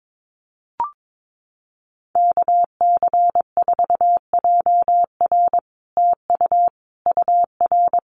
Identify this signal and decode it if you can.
Morse code, heard as a keyed tone, about 22 wpm, 700 Hz. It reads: KC4JR TU UR
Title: Common QSO Phrases Encore 22wpm